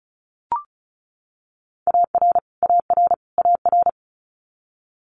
Morse code, heard as a keyed tone, about 36 wpm, 700 Hz. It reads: AR AR AR
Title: Multi-letter Phonemes Repeated 3x 35wpm